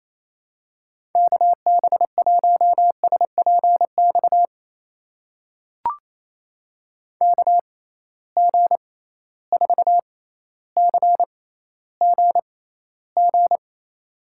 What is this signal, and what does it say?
Morse code, heard as a keyed tone, about 28 wpm, spaced as for 14 wpm, 700 Hz. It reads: KB1SPX KG4CGG
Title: USA Call Signs - 6 Characters - 6x Character Spacing - 28wpm